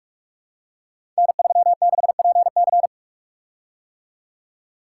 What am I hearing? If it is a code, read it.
Morse code, heard as a keyed tone, about 44 wpm, 700 Hz. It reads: N36PC
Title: Code Groups - 5 Letter:Numbers - 45wpm